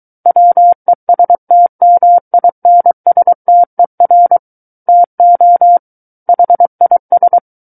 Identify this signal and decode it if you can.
Morse code, heard as a keyed tone, about 23 wpm, 700 Hz. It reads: WESTMINSTER TO HIS
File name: Book One / 14-Chapter 14 - 23 WPM